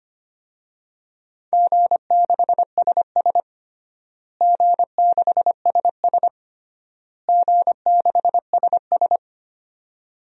Morse code, heard as a keyed tone, about 25 wpm, 700 Hz. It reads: G6SS G6SS G6SS
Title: DXCC-100 Call Signs - 4 Characters - Repeated 3x - 25wpm